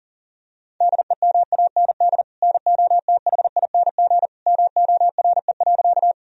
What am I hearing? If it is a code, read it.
Morse code, heard as a keyed tone, about 40 wpm, 700 Hz. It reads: DEMAND NOTHING MORE.
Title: Sentences from Top 1000 Words - Review Part 4 - 40wpm